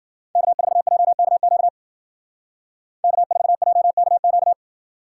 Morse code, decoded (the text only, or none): K4JKQ K4JKQ